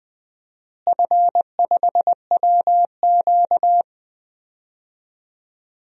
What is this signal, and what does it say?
Morse code, heard as a keyed tone, about 20 wpm, 700 Hz. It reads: F5WQ
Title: DXCC-100 Call Signs - 4 Characters - Repeated 2x - 20wpm